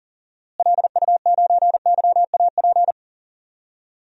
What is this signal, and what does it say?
Morse code, heard as a keyed tone, about 40 wpm, 700 Hz. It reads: LU9YAP
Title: DXCC-100 Call Signs - 6 Characters - Repeated 3x - 40wpm